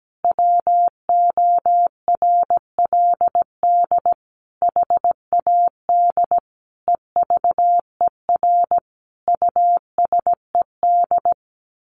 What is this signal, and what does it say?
Morse code, heard as a keyed tone, about 17 wpm, 700 Hz. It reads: WORLD HAD EVER USED
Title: Sentences from Top 1100 Words 17wpm